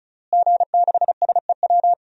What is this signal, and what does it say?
Morse code, heard as a keyed tone, about 36 wpm, 700 Hz. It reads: G6SEW
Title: DXCC-100 Call Signs - 5 Characters - Repeated 2x - 35wpm